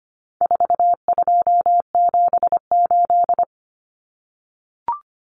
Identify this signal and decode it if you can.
Morse code, heard as a keyed tone, about 25 wpm, 700 Hz. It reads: 4278
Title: Numbers - 4-Digit 25wpm